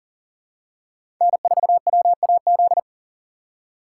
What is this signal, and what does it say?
Morse code, heard as a keyed tone, about 40 wpm, 700 Hz. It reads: N4WAZ
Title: USA Call Signs - 5 Characters - 2x Character Spacing - 40wpm